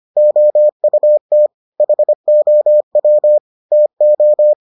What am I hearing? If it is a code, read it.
Morse code, heard as a keyed tone, about 25 wpm, 600 Hz. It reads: OUT HOW TO